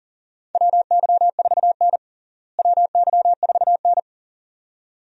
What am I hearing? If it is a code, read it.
Morse code, heard as a keyed tone, about 40 wpm, 700 Hz. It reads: WY4N WY4N
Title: USA Call Signs - 4 Characters - Repeated 2x - 40wpm